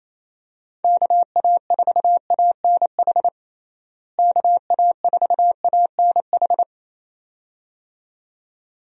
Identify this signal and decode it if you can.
Morse code, heard as a keyed tone, about 28 wpm, 700 Hz. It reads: KA4ANH KA4ANH
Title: USA Call Signs - 6 Characters - Repeated 2x - 28wpm